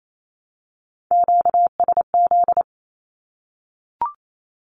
Morse code, heard as a keyed tone, about 28 wpm, 700 Hz. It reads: QSZ
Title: Q-Codes Repeated 3x 28wpm